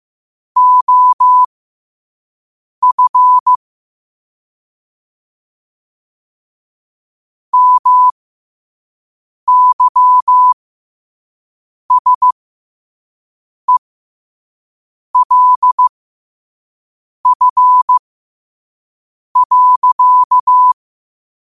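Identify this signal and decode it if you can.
Morse code, heard as a keyed tone, about 15 wpm, spaced as for 5 wpm, 1000 Hz. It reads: OF MYSELF.
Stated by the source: Morse code practice at 5 wpm.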